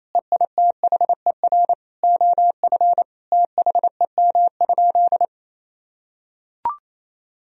Morse code, heard as a keyed tone, about 27 wpm, 700 Hz. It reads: EITHER OF THEM?
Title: Sentences from Top 1000 Words - Review Part 4 - 28wpm